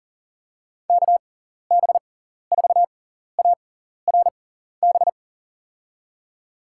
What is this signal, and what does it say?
Morse code, heard as a keyed tone, about 38 wpm, spaced as for 14 wpm, 700 Hz. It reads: KB4ARB
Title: USA Call Signs - 6x Character Spacing - 40wpm